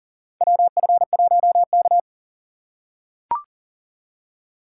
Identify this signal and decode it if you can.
Morse code, heard as a keyed tone, about 39 wpm, 700 Hz. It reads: WF1K
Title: USA Call Signs - 4 Characters - 3x Character Spacing - 40wpm